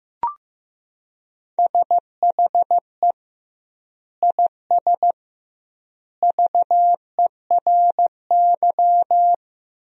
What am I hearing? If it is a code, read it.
Morse code, heard as a keyed tone, about 15 wpm, 700 Hz. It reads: SHE IS VERY
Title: 2x Word-Spacing - Sentences from Top 400 Words - 15wpm